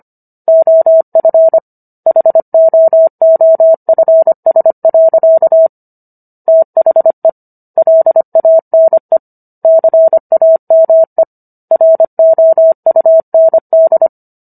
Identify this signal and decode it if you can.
Morse code, heard as a keyed tone, about 25 wpm, 650 Hz. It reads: OF HOOFS. THE LANE CAME ROUND